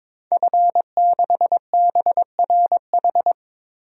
Morse code, heard as a keyed tone, about 22 wpm, 700 Hz. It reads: F6BRH
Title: DXCC-100 Call Signs - 5 Characters - 22wpm